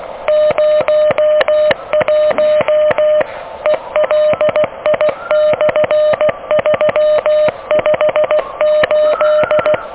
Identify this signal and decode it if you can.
Morse code, heard as a keyed tone, about 19 wpm, 610 Hz. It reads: 01 ELI/358
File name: OK QRP 2023 / OK QRP 2023 OK1DWF